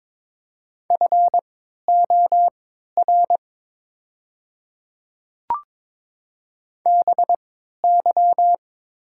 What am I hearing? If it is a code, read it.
Morse code, heard as a keyed tone, about 23 wpm, spaced as for 13 wpm, 700 Hz. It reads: FOR BY
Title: 3x Character Spacing - Top 100 Words - 22wpm